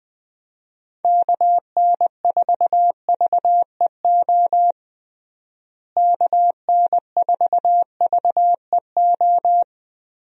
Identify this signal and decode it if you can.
Morse code, heard as a keyed tone, about 20 wpm, 700 Hz. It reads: KN4VEO KN4VEO
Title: USA Call Signs - 6 Characters - Repeated 3x - 20wpm